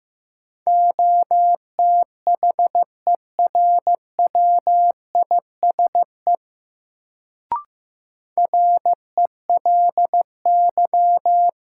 Morse code, heard as a keyed tone, about 15 wpm, 700 Hz. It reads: OTHERWISE RELY
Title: Sentences from Top 1200 Words - New Word Drill - 15wpm